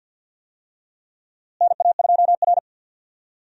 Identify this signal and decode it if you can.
Morse code, heard as a keyed tone, about 46 wpm, 700 Hz. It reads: NA2R
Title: USA Call Signs - 4 Characters - Repeated 3x - 50wpm